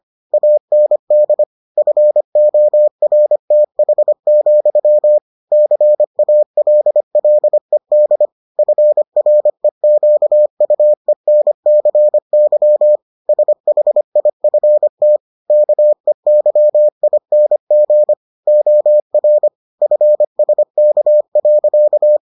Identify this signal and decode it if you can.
Morse code, heard as a keyed tone, about 25 wpm, 600 Hz. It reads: AND FORTH, CALLED FREQUENCY SHIFT KEYING OR FSK.